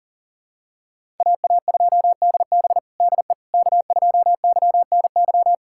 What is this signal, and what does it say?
Morse code, heard as a keyed tone, about 37 wpm, 700 Hz. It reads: AA2DB DE K2YNY
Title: USA Callsign DE Callsign - 5 Characters - 40wpm